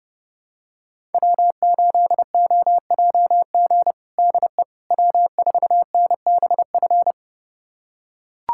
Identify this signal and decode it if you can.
Morse code, heard as a keyed tone, about 32 wpm, 700 Hz. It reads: W8OJG DE W4NBF
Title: USA Callsign DE Callsign - 5 Characters - 30wpm